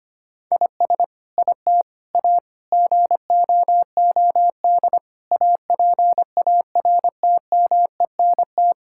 Morse code, heard as a keyed tone, about 25 wpm, 700 Hz. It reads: IS IT A GOOD APARTMENT
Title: Sentences from Top 1000 Words - Review Part 6 - 25wpm